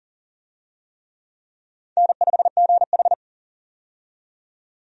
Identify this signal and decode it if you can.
Morse code, heard as a keyed tone, about 40 wpm, 700 Hz. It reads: N5GH
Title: USA Call Signs - 4 Characters - 40wpm